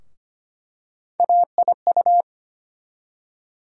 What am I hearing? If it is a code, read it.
Morse code, heard as a keyed tone, about 25 wpm, 700 Hz. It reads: AIU
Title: USA Call Sign Postfixes - 3 Letters 25wpm